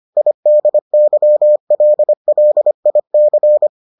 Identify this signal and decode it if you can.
Morse code, heard as a keyed tone, about 25 wpm, 600 Hz. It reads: IDYLLIC